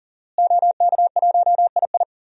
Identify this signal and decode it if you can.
Morse code, heard as a keyed tone, about 40 wpm, 700 Hz. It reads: OK1II
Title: DXCC-100 Call Signs - 5 Characters - Repeated 3x - 40wpm